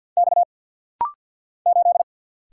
Morse code, heard as a keyed tone, about 48 wpm, 700 Hz. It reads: X 8
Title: Single Letter-Number - Rapid-Fire 50wpm